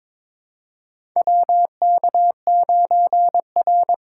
Morse code, heard as a keyed tone, about 22 wpm, 700 Hz. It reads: WK9R
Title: USA Call Signs - 4 Characters - 4x Character Spacing - 22wpm